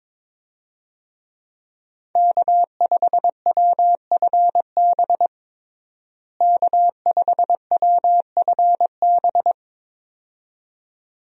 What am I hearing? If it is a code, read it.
Morse code, heard as a keyed tone, about 22 wpm, 700 Hz. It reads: K5WFB K5WFB
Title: USA Call Signs - 5 Characters - Repeated 2x - 22wpm